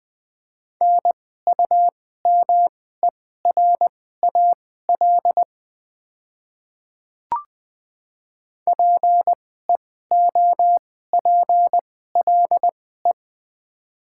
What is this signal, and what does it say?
Morse code, heard as a keyed tone, about 20 wpm, spaced as for 15 wpm, 700 Hz. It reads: NUMERAL PEOPLE
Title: 2x Character Spacing - Top 500 Words - 20wpm